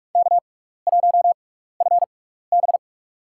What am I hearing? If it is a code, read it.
Morse code, heard as a keyed tone, about 45 wpm, spaced as for 15 wpm, 700 Hz. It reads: K1FB
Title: USA Call Signs - 4 Characters - 6x Character Spacing - 45wpm